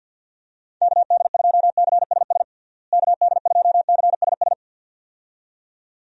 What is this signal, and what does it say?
Morse code, heard as a keyed tone, about 52 wpm, 700 Hz. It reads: KD2CSS KD2CSS
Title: USA Call Signs - 6 Characters - Repeated 2x - 50wpm